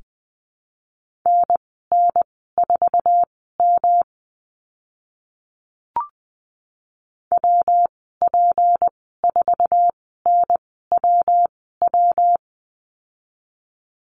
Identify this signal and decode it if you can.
Morse code, heard as a keyed tone, about 20 wpm, spaced as for 14 wpm, 700 Hz. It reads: NN4M WP4NWW
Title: 154 Lesson 25g - Letter p - Call Signs 20wpm